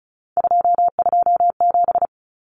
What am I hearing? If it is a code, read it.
Morse code, heard as a keyed tone, about 34 wpm, 700 Hz. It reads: 227
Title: Common Cut Numbers - 3-Digit 35wpm